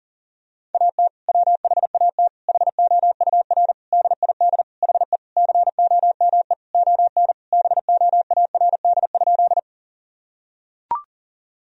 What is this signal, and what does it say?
Morse code, heard as a keyed tone, about 40 wpm, 700 Hz. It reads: AT WHAT HOUR DID HE COME ON BOARD?
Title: Sentences from Top 1000 Words - Review Part 4 - 40wpm